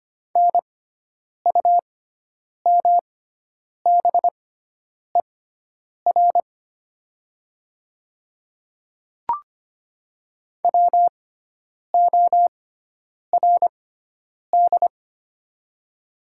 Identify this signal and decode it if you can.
Morse code, heard as a keyed tone, about 25 wpm, spaced as for 9 wpm, 700 Hz. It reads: NUMBER WORD
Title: 6x Character Spacing - Top 200 Words - 25wpm